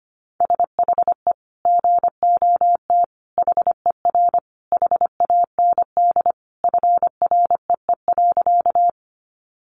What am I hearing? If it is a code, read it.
Morse code, heard as a keyed tone, about 25 wpm, 700 Hz. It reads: SHE GOT HER HAND FREE.
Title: Sentences from Top 500 Words 25wpm